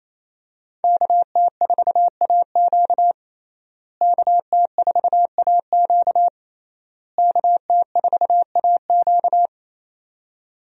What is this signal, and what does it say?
Morse code, heard as a keyed tone, about 28 wpm, 700 Hz. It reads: KT4AQ KT4AQ KT4AQ
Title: USA Call Signs - 5 Characters - Repeated 3x - 28wpm